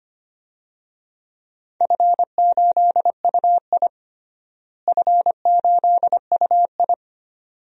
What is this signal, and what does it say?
Morse code, heard as a keyed tone, about 25 wpm, 700 Hz. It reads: F8UI F8UI
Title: DXCC-100 Call Signs - 4 Characters - Repeated 2x - 25wpm